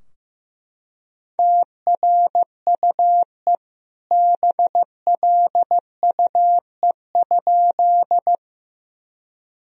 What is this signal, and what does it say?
Morse code, heard as a keyed tone, about 15 wpm, 700 Hz. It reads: TRUE BLUE?
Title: Sentences from Top 1000 Words - Review Part 3 - 15wpm